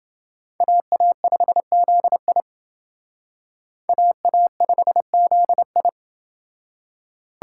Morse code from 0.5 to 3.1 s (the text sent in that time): AA5ZI